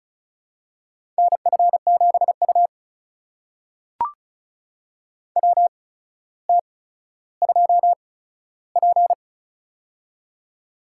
Morse code, heard as a keyed tone, about 36 wpm, spaced as for 15 wpm, 700 Hz. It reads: NF7U WT2P
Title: USA Call Signs - 4 Characters - 8x Character Spacing - 35wpm